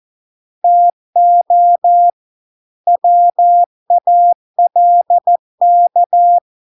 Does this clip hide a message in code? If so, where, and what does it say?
Yes: Morse code from 0.6 to 6.4 s, heard as a keyed tone, about 14 wpm, 700 Hz. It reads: TO WALK